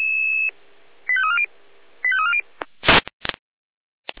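Five-tone selective call (ZVEI), heard as quick runs of five tones, tones of about 70 ms. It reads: 85340 85340
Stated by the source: Five-tone ZVEI selective call 85349 received via SDR.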